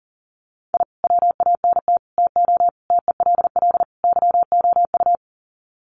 Morse code, heard as a keyed tone, about 40 wpm, 700 Hz. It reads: I WANT TO TELL YOU
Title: Sets of 5 Words Encore 40wpm